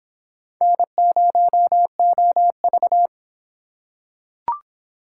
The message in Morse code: N0OV

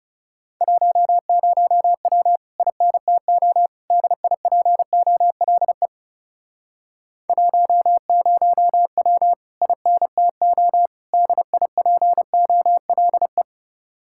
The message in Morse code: 10W INTO DIPOLE 10W INTO DIPOLE

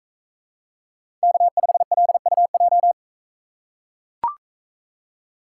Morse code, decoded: K5LUJ

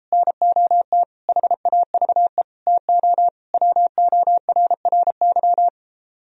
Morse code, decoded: NOT HAVE TO WORRY

